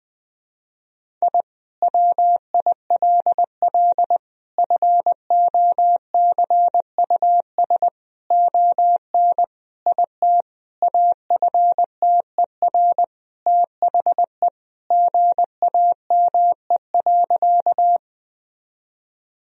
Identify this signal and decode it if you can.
Morse code, heard as a keyed tone, about 20 wpm, 700 Hz. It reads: I WILL FOCUS ON IT AFTER THE GAME.
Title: Sentences from Top 1000 Words - Review Part 4 - 20wpm